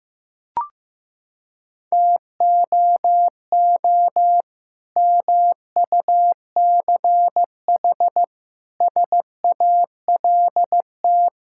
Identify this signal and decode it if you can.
Morse code, heard as a keyed tone, about 16 wpm, 700 Hz. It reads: TOO MUCH SALT
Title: Sentences from Top 1000 Words - Review Part 5 - 15wpm